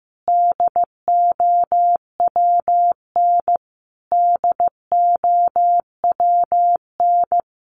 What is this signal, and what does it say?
Morse code, heard as a keyed tone, about 15 wpm, 700 Hz. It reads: DOWN DOWN
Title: Top 500 Words Repeated 15wpm